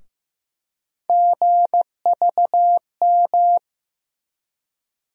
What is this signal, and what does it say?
Morse code, heard as a keyed tone, about 16 wpm, 700 Hz. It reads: GVM
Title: USA Call Sign Postfixes - 3 Letters 15wpm